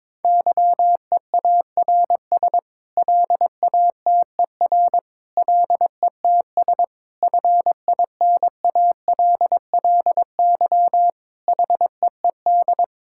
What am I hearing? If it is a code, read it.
Morse code, heard as a keyed tone, about 22 wpm, 700 Hz. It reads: YEARS LATER LETS FINALLY HEED
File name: practice / 22wpm / Sets of 8 Words - 22wpm